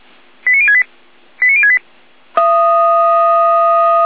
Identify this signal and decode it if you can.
Five-tone selective call (ZVEI), heard as quick runs of five tones, tones of about 70 ms. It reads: 89068 89068
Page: /455/SDR/ZVEI/89201/